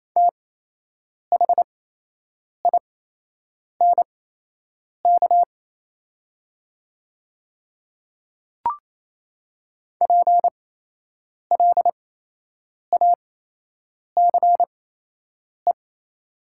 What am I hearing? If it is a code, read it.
Morse code, heard as a keyed tone, about 28 wpm, spaced as for 8 wpm, 700 Hz. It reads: THINK PLACE
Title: 8x Character Spacing - Top 200 Words - 28wpm